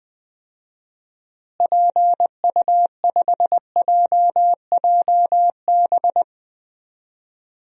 Morse code, heard as a keyed tone, about 20 wpm, 700 Hz. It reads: PU5JJB
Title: DXCC-100 Call Signs - 6 Characters - Repeated 3x - 20wpm